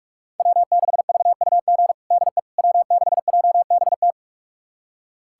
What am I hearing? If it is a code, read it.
Morse code, heard as a keyed tone, about 46 wpm, 700 Hz. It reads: W6VUG DE W6JBT
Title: USA Callsign DE Callsign - 5 Characters - 45wpm